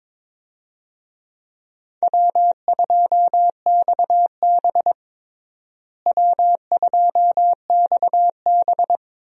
Morse code, heard as a keyed tone, about 22 wpm, 700 Hz. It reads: W2XB W2XB
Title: USA Call Signs - 4 Characters - Repeated 3x - 22wpm